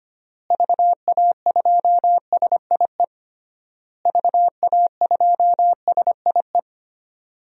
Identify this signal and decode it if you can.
Morse code, heard as a keyed tone, about 25 wpm, 700 Hz. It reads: VA2SIE VA2SIE
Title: DXCC-100 Call Signs - 6 Characters - Repeated 3x - 25wpm